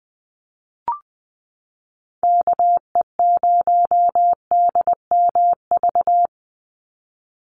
Morse code, heard as a keyed tone, about 20 wpm, 700 Hz. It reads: KE0DMV
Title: USA Call Signs 20wpm